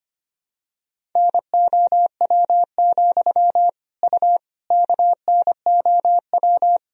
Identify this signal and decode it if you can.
Morse code, heard as a keyed tone, about 25 wpm, 700 Hz. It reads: NOW, U KNOW
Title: Sentences from Top 200 Words and Abbreviations 25wpm